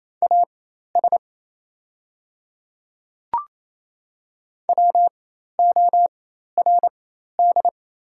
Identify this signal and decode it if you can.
Morse code, heard as a keyed tone, about 28 wpm, spaced as for 13 wpm, 690 Hz. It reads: AS WORD
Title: 4x Character Spacing - Top 100 Words - 28wpm